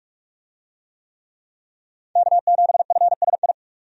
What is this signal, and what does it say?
Morse code, heard as a keyed tone, about 44 wpm, 700 Hz. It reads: K7FSI
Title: USA Call Signs - 5 Characters - 45wpm